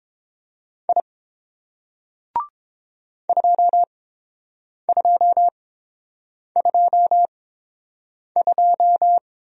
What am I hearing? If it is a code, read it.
Morse code, heard as a keyed tone, about 30 wpm, spaced as for 7 wpm, 700 Hz. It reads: I 2222